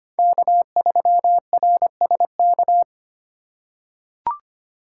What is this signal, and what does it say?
Morse code, heard as a keyed tone, about 25 wpm, 700 Hz. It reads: K3RSK
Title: USA Call Signs - 5 Characters - 4x Character Spacing - 25wpm